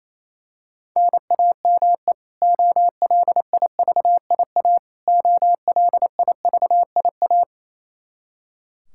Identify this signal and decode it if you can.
Morse code, heard as a keyed tone, about 27 wpm, 700 Hz. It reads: NAME OLIVIA OLIVIA
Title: Practice QSO - Exchange 1 - Beginning - 28wpm